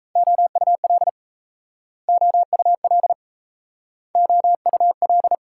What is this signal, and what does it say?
Morse code, heard as a keyed tone, about 37 wpm, 700 Hz. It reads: OUL OUL OUL